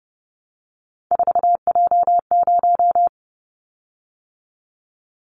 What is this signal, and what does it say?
Morse code, heard as a keyed tone, about 30 wpm, 700 Hz. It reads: 4J0
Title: Callsign Prefixes for Top 100 DXCC - 30wpm